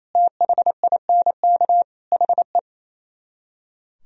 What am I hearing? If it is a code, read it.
Morse code, heard as a keyed tone, about 31 wpm, 700 Hz. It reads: THINK HE
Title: Speed-Racing - Sets of 5 Words 28wpm